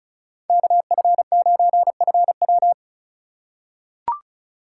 Morse code, heard as a keyed tone, about 34 wpm, 700 Hz. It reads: KF9FW
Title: USA Call Signs - 2x Character Spacing - 35wpm